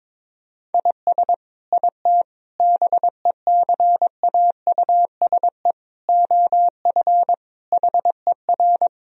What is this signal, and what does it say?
Morse code, heard as a keyed tone, about 22 wpm, 700 Hz. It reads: IS IT BECAUSE OF HER
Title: Sentences from Top 1000 Words - Review Part 3 - 22wpm